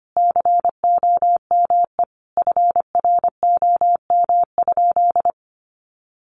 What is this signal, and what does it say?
Morse code, heard as a keyed tone, about 25 wpm, 700 Hz. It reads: COME FROM?
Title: Sentences from Top 300 Words 25wpm